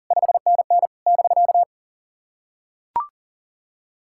Morse code, heard as a keyed tone, about 43 wpm, 700 Hz. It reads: 5NN <BK>
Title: SOTA QSO Practice 40wpm